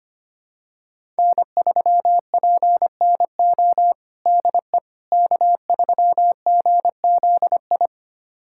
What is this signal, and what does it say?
Morse code, heard as a keyed tone, about 25 wpm, 700 Hz. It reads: N3PNO DE K3GZI
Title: USA Callsign DE Callsign - 5 Characters - 25wpm